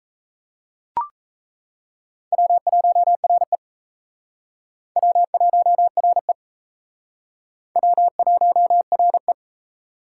Morse code, heard as a keyed tone, about 34 wpm, 700 Hz. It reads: W1RE W1RE W1RE
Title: Speed Racing - USA Call Signs 28wpm